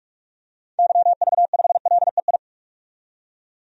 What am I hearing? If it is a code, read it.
Morse code, heard as a keyed tone, about 46 wpm, 700 Hz. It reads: YV5LEI
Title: DXCC-100 Call Signs - 6 Characters - 45wpm